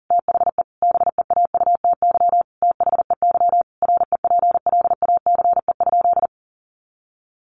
Morse code, heard as a keyed tone, about 40 wpm, 700 Hz. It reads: THE BEAUTY THEY REPLACE?
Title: Sentences from Top 1400 Words 40wpm